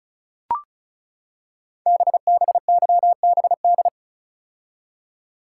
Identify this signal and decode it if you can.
Morse code, heard as a keyed tone, about 36 wpm, 700 Hz. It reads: BBYBD
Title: Code Groups - 5 Letters - 35wpm